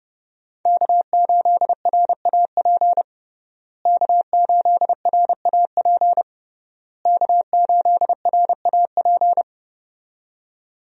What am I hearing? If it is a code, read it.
Morse code, heard as a keyed tone, about 30 wpm, 700 Hz. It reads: K8RAP K8RAP K8RAP
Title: USA Call Signs - 5 Characters - Repeated 3x - 30wpm